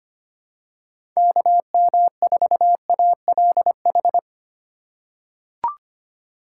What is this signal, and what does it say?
Morse code, heard as a keyed tone, about 25 wpm, 700 Hz. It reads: KM4ALH
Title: USA Call Signs - 6 Characters - 3x Character Spacing - 25wpm